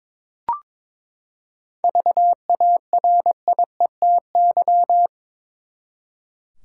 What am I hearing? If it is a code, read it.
Morse code, heard as a keyed tone, about 21 wpm, 700 Hz. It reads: VARIETY
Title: Seven Letter Words - 22wpm